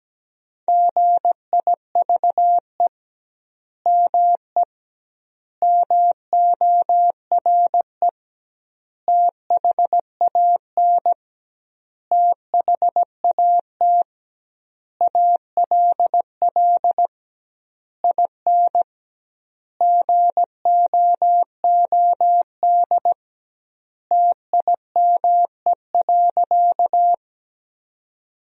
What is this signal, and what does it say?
Morse code, heard as a keyed tone, about 17 wpm, 700 Hz. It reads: GIVE ME MORE THAN THAT ALL IN GOOD TIME.
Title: 2x Word-Spacing - Sentences from Top 200 Words - 17wpm